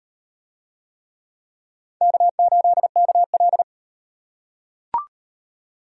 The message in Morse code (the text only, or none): K8KL